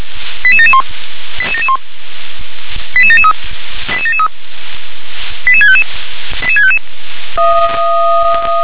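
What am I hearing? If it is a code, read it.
Five-tone selective call (ZVEI), heard as quick runs of five tones, tones of about 70 ms. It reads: 88771 88771 88773 88773 88655 88655
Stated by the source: ZVEI five-tone selective call received on SDR.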